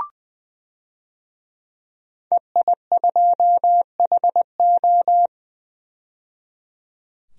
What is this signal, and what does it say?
Morse code, heard as a keyed tone, about 20 wpm, 700 Hz. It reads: EI2HO
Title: DXCC-100 Call Signs - 5 Characters - 20wpm